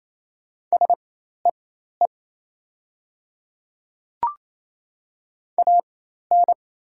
Morse code, heard as a keyed tone, about 28 wpm, spaced as for 13 wpm, 700 Hz. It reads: SEE AN
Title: 4x Character Spacing - Top 100 Words - 28wpm